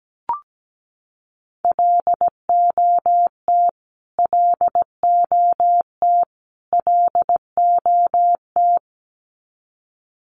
Morse code, heard as a keyed tone, about 17 wpm, 700 Hz. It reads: LOT LOT LOT
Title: Top 500 Words Repeated 17wpm